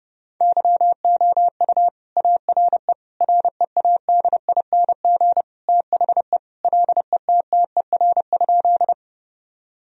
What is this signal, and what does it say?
Morse code, heard as a keyed tone, about 29 wpm, 700 Hz. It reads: YOU ARE READING THE LETTER?
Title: Sentences from Top 1000 Words - Review Part 3 - 30wpm